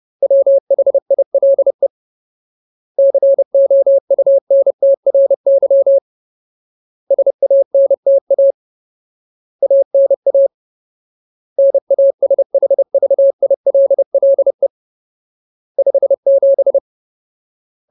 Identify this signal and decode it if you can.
Morse code, heard as a keyed tone, about 30 wpm, 550 Hz. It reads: WHILE COUNTRY SANTA ANA NASHVILLE 57
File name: Ultimate Word List / 30WPM Ultimate